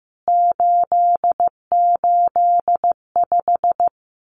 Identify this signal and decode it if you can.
Morse code, heard as a keyed tone, about 15 wpm, 700 Hz. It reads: 885
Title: CWT Exchange 15wpm